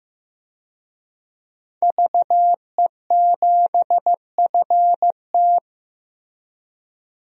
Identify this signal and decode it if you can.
Morse code, heard as a keyed tone, about 15 wpm, 700 Hz. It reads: VE7FT
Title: DXCC-100 Call Signs - 5 Characters - Repeated 2x - 15wpm